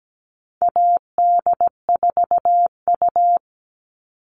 Morse code, heard as a keyed tone, about 17 wpm, 700 Hz. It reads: AD4U
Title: All Cut Numbers - 4-Digit 17wpm